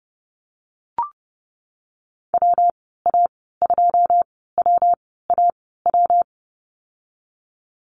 Morse code, heard as a keyed tone, about 30 wpm, spaced as for 17 wpm, 700 Hz. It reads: WA2WAW
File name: Lesson 19g - Letter w - Call Signs 30wpm